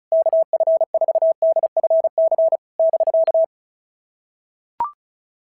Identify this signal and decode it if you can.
Morse code, heard as a keyed tone, about 36 wpm, 650 Hz. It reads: KF4DFC <BK>
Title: SOTA QSO Practice 35wpm